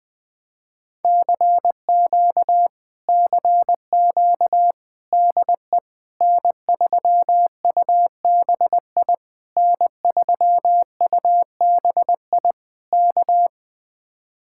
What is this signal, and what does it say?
Morse code, heard as a keyed tone, about 20 wpm, 700 Hz. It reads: CQ CQ DE N3UBI N3UBI K
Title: Speed-Racing Common QSO Phrases - 20wpm